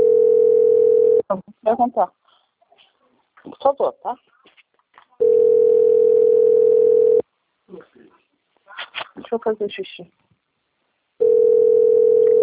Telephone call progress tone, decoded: ringback tone

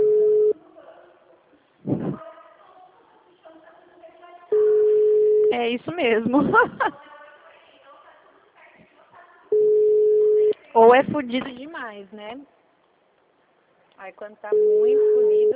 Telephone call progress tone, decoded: ringback tone

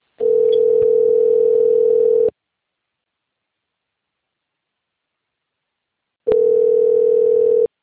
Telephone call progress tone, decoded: ringback tone

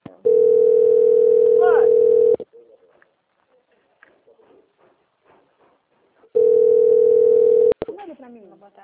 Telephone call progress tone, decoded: ringback tone

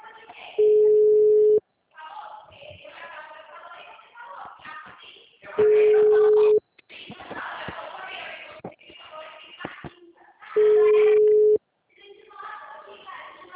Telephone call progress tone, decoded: ringback tone

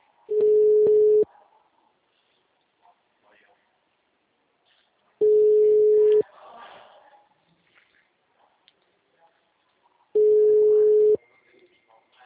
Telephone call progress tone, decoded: ringback tone

